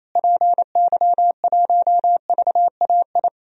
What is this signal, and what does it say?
Morse code, heard as a keyed tone, about 28 wpm, 700 Hz. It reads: PY1VAI
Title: DXCC-100 Call Signs - 6 Characters - Repeated 2x - 28wpm